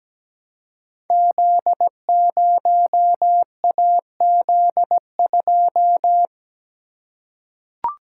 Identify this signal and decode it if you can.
Morse code, heard as a keyed tone, about 17 wpm, 700 Hz. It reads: Z0AZ2
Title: Code Groups - 5 Letter:Numbers - 17wpm